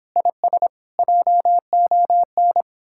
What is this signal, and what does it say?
Morse code, heard as a keyed tone, about 29 wpm, 700 Hz. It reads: IS JON